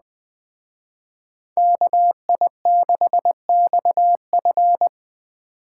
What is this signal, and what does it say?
Morse code, heard as a keyed tone, about 21 wpm, 700 Hz. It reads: KI6XF